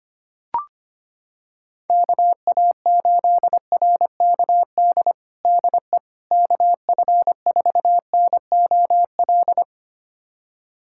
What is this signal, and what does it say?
Morse code, heard as a keyed tone, about 25 wpm, 700 Hz. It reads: KA8RKD DE KF4NOL
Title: USA Callsign DE Callsign - 6 Characters - 25wpm